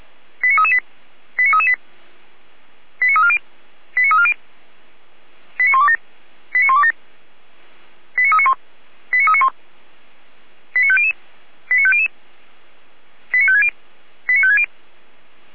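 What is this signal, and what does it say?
Five-tone selective call (ZVEI), heard as quick runs of five tones, tones of about 70 ms. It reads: 89338 89338 89340 89340 89127 89127 89391 89391 89500 89500 89560 89560
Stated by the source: ZVEI five-tone call 89310 received by SDR.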